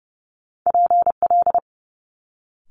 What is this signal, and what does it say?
Morse code, heard as a keyed tone, about 30 wpm, 700 Hz. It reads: PL
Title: Multi-letter Phonemes Quick Fire 30wpm